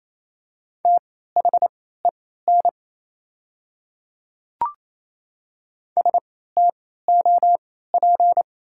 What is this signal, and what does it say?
Morse code, heard as a keyed tone, about 28 wpm, spaced as for 16 wpm, 690 Hz. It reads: THEN STOP
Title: 3x Character Spacing - Top 500 Words - 28wpm